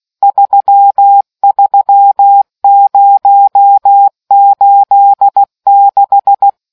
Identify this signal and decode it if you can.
Morse code, heard as a keyed tone, about 16 wpm, 800 Hz. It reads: 33086